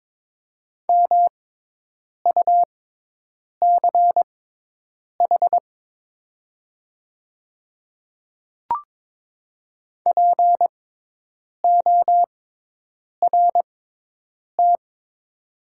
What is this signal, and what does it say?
Morse code, heard as a keyed tone, about 22 wpm, spaced as for 8 wpm, 700 Hz. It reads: MUCH PORT
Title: 6x Character Spacing - Top 200 Words - 22wpm